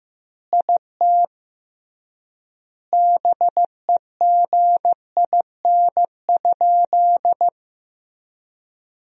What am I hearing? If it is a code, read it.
Morse code, heard as a keyed tone, about 15 wpm, 700 Hz. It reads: IT BEGIN?
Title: 3x Word-Spacing - Sentences from Top 300 Words - 15wpm